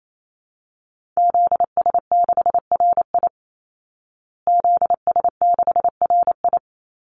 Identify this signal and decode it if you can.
Morse code, heard as a keyed tone, about 28 wpm, 700 Hz. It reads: ZS6RI ZS6RI
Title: Call Signs Repeated 3x - 28wpm